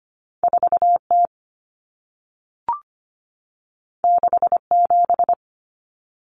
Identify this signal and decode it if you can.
Morse code, heard as a keyed tone, about 25 wpm, 700 Hz. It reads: 4T 67
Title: Common Cut Numbers - 2-Digit 25wpm